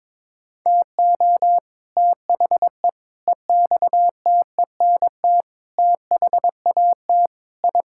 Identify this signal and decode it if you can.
Morse code, heard as a keyed tone, about 22 wpm, 700 Hz. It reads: TO THE EXTENT THAT I
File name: practice / 22wpm / Sentences with 4-Word Phrases - 22wpm